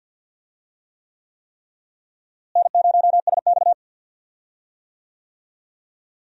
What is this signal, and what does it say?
Morse code, heard as a keyed tone, about 51 wpm, 700 Hz. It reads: N0SX
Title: USA Call Signs - 4 Characters - Repeated 3x - 50wpm